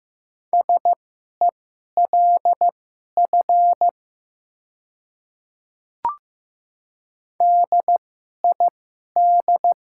Morse code, heard as a keyed tone, about 15 wpm, spaced as for 11 wpm, 700 Hz. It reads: SELF DID
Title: 2x Character Spacing - Top 200 Words - 15wpm